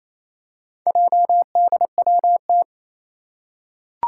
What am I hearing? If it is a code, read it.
Morse code, heard as a keyed tone, about 28 wpm, 700 Hz. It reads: JDWT
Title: Four Challenging Letter-Number Combinations 28wpm